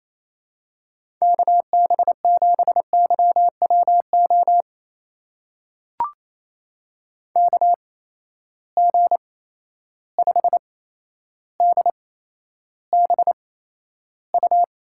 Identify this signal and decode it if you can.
Morse code, heard as a keyed tone, about 28 wpm, spaced as for 12 wpm, 700 Hz. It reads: KB7YWO KG5DBU
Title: USA Call Signs - 6 Characters - 8x Character Spacing - 28wpm